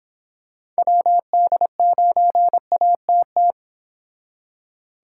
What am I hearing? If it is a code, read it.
Morse code, heard as a keyed tone, about 26 wpm, 700 Hz. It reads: WD9ATT